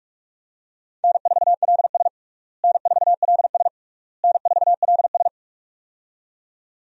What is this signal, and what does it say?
Morse code, heard as a keyed tone, about 47 wpm, 700 Hz. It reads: N4LS N4LS N4LS
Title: USA Call Signs - 4 Characters - Repeated 3x - 45wpm